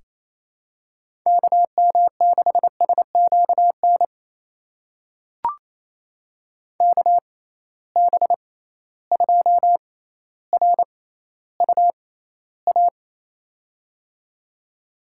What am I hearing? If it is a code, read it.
Morse code, heard as a keyed tone, about 28 wpm, spaced as for 14 wpm, 700 Hz. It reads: KM6SQN KB2RUA
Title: USA Call Signs - 6 Characters - 6x Character Spacing - 28wpm